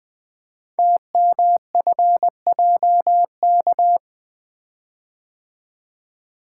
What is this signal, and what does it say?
Morse code, heard as a keyed tone, about 20 wpm, 700 Hz. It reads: TMFJK
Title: Code Groups - 5 Letter:Numbers - 20wpm